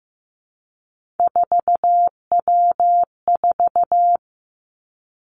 Morse code, heard as a keyed tone, about 15 wpm, 700 Hz. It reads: 4W4
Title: Three Challenging Letter-Number Combinations 15wpm